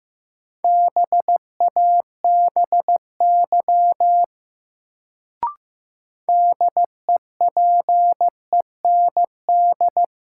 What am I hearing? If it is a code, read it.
Morse code, heard as a keyed tone, about 15 wpm, 700 Hz. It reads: BABY DEPEND
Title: Sentences from Top 1300 Words - New Word Drill - 15wpm